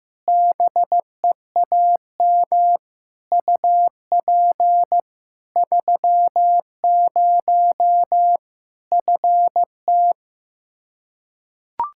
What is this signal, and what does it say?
Morse code, heard as a keyed tone, about 15 wpm, 700 Hz. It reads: BEAM UP 30 FT
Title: Practice QSO - Exchange 2 - Info - 15wpm